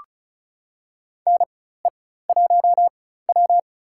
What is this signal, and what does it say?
Morse code, heard as a keyed tone, about 34 wpm, spaced as for 16 wpm, 700 Hz. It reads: NE1W
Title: USA Call Signs - 4 Characters - 4x Character Spacing - 35wpm